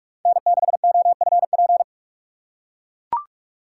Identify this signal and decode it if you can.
Morse code, heard as a keyed tone, about 44 wpm, 700 Hz. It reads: N6OFP
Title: USA Call Signs - 5 Characters - 4x Character Spacing - 45wpm